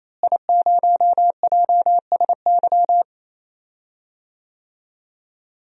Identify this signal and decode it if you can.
Morse code, heard as a keyed tone, about 28 wpm, 700 Hz. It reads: I0JSY
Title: DXCC-100 Call Signs - 5 Characters - Repeated 3x - 28wpm